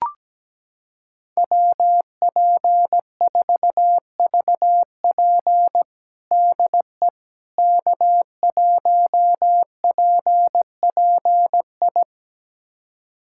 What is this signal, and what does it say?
Morse code, heard as a keyed tone, about 17 wpm, 700 Hz. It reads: WP4VP DE K1PPI
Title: USA Callsign DE Callsign - 5 Characters - 17wpm